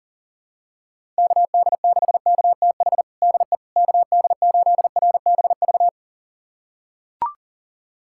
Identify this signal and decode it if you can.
Morse code, heard as a keyed tone, about 40 wpm, 700 Hz. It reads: KD6KTH DE KD8RBV
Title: USA Callsign DE Callsign - 6 Characters - 40wpm